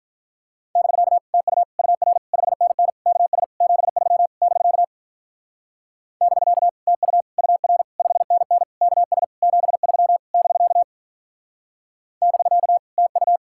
Morse code, heard as a keyed tone, about 46 wpm, 700 Hz. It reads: <BK> TU UR 5NN KS 73 <BK> <BK> TU UR 5NN KS 73 <BK> <BK> TU